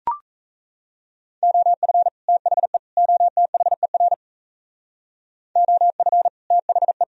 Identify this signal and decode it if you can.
Morse code, heard as a keyed tone, about 34 wpm, 700 Hz. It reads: OF THE OTHER OF THE
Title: Speed-Racing - Sets of 3 Words 28wpm